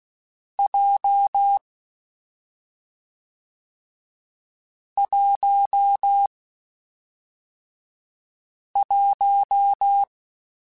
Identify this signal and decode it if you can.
Morse code, heard as a keyed tone, about 16 wpm, 800 Hz. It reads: J 1 1
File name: Short Runs / Short Runs-mp3s / awj1ndb6gz7 @5